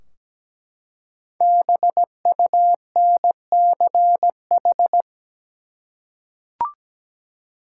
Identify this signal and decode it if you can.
Morse code, heard as a keyed tone, about 17 wpm, 700 Hz. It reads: BUNCH
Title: Five Letter Words - 17wpm